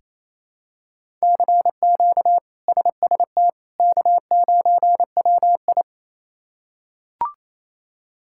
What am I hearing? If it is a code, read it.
Morse code, heard as a keyed tone, about 28 wpm, 700 Hz. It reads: CQ SST K9WI
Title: K1USN SST QSO Practice - 28wpm